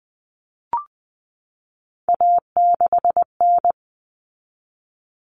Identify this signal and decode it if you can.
Morse code, heard as a keyed tone, about 20 wpm, 700 Hz. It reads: A6N
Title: All Cut Numbers - 3-Digit 20wpm